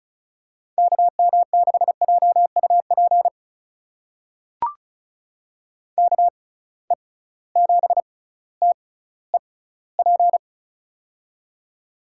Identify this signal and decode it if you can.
Morse code, heard as a keyed tone, about 36 wpm, spaced as for 18 wpm, 700 Hz. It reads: KM6JUP KE7TEP
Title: USA Call Signs - 6 Characters - 6x Character Spacing - 35wpm